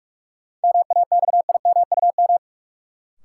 Morse code, heard as a keyed tone, about 45 wpm, 700 Hz. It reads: MAXIMUM
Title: Sentences from Top 1300 Words - New Word Drill Alt - 45wpm